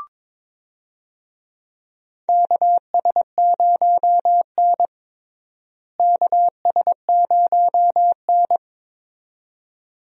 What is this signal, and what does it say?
Morse code, heard as a keyed tone, about 22 wpm, 700 Hz. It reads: KS0N KS0N
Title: USA Call Signs - 4 Characters - Repeated 2x - 22wpm